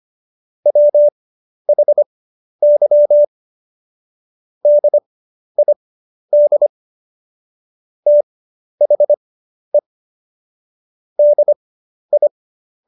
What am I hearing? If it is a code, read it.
Morse code, heard as a keyed tone, about 25 wpm, spaced as for 11 wpm, 590 Hz. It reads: WHY DID THE DI
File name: Jokes-25-char-10-eff